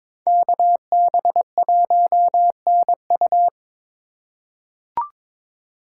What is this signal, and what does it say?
Morse code, heard as a keyed tone, about 22 wpm, 700 Hz. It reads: KB1NU
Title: USA Call Signs - 5 Characters - 3x Character Spacing - 22wpm